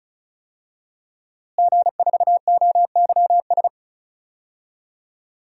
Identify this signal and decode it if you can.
Morse code, heard as a keyed tone, about 36 wpm, 700 Hz. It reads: G4OYS
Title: DXCC-100 Call Signs - 5 Characters - Repeated 2x - 35wpm